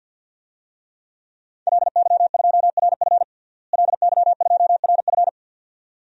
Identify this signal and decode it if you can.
Morse code, heard as a keyed tone, about 51 wpm, 700 Hz. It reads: LY2RF LY2RF
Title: DXCC-100 Call Signs - 5 Characters - Repeated 2x - 50wpm